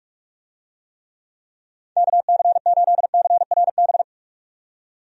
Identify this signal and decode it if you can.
Morse code, heard as a keyed tone, about 44 wpm, 700 Hz. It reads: KC8CRB
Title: USA Call Signs - 6 Characters - 45wpm